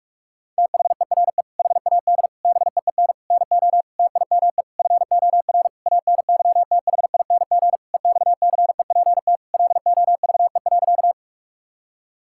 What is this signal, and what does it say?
Morse code, heard as a keyed tone, about 46 wpm, 700 Hz. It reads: THERE HAD BEEN NO TIME FOR ANYTHING EXCEPT LOVE.
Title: Sentences from Top 1600 Words 45wpm